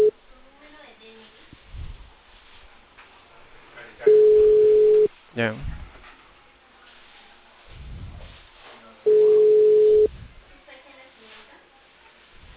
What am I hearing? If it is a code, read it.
Ringback tone (Europe, 425 Hz), with speech.